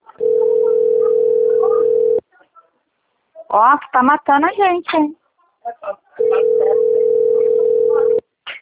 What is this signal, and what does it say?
Ringback tone (North America), 440 + 480 Hz, with speech.
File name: rg-1600-unknown- - - .15645